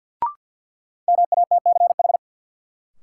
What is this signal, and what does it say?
Morse code, heard as a keyed tone, about 52 wpm, 700 Hz. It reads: MATCH
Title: Sentences from Top 1600 Words - New Word Drill - 50wpm